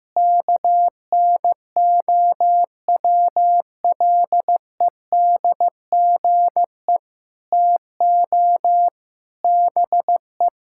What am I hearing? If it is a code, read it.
Morse code, heard as a keyed tone, about 15 wpm, 700 Hz. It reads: KNOWLEDGE TO BE
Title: Sentences from Top 1000 Words - Review Part 6 - 15wpm